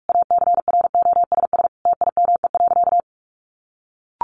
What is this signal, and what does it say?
Morse code, heard as a keyed tone, about 44 wpm, 700 Hz. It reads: ACROSS TIME.
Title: A Sound Of Thunder 45wpm